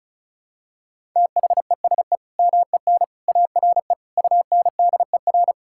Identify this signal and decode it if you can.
Morse code, heard as a keyed tone, about 36 wpm, 700 Hz. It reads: THESE MEN ARE UNDER
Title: Sentences from Top 1000 Words - Review Part 6 - 35wpm